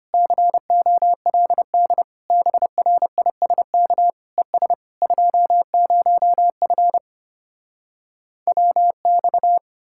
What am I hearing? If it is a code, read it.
Morse code, heard as a keyed tone, about 33 wpm, 700 Hz. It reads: COLD BRISK ES 20F WX